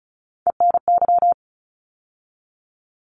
Morse code, heard as a keyed tone, about 36 wpm, 700 Hz. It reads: ENY
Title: ARRL Sections 35wpm